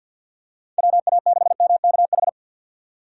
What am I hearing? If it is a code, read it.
Morse code, heard as a keyed tone, about 53 wpm, 690 Hz. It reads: WA6MKH
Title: USA Call Signs - 6 Characters - 8x Character Spacing - 50wpm